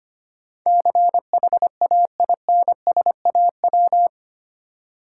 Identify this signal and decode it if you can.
Morse code, heard as a keyed tone, about 25 wpm, 700 Hz. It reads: CHAINSAW
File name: practice / 25wpm / Difficult Words with Hints - 25wpm